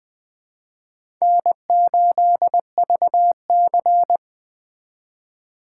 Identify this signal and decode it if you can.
Morse code, heard as a keyed tone, about 20 wpm, 700 Hz. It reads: N8VC
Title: Speed Racing - USA 1x2 Call Signs 17wpm